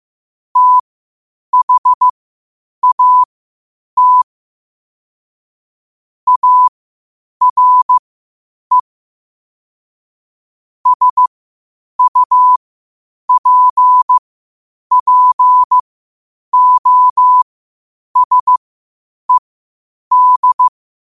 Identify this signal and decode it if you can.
Morse code, heard as a keyed tone, about 15 wpm, spaced as for 8 wpm, 1000 Hz. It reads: THAT ARE SUPPOSED
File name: qotdfiles / 8wpm